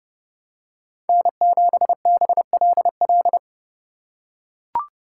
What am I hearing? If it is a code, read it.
Morse code, heard as a keyed tone, about 31 wpm, 700 Hz. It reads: N7BLL
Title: USA Call Signs - 5 Characters - 8x Character Spacing - 30wpm